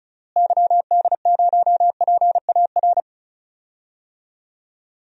Morse code, heard as a keyed tone, about 34 wpm, 700 Hz. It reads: YD0PAR
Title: DXCC-100 Call Signs - 6 Characters - Repeated 2x - 35wpm